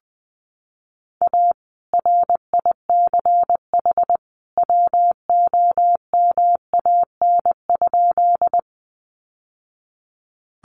Morse code, heard as a keyed tone, about 20 wpm, 700 Hz. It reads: A RICH WOMAN?
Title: Sentences from Top 1100 Words 20wpm